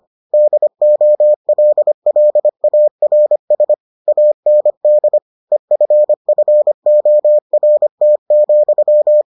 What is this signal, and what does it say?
Morse code, heard as a keyed tone, about 25 wpm, 600 Hz. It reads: DOLLARS AND EFFORT,